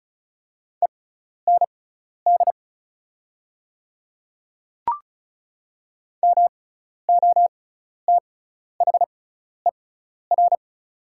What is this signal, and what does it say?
Morse code, heard as a keyed tone, about 36 wpm, spaced as for 12 wpm, 690 Hz. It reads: END MOTHER
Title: 6x Character Spacing - Top 200 Words - 35wpm